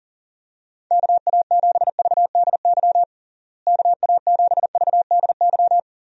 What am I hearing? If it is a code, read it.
Morse code, heard as a keyed tone, about 37 wpm, 700 Hz. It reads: KA7VDY KA7VDY
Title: USA Call Signs - 6 Characters - Repeated 2x - 40wpm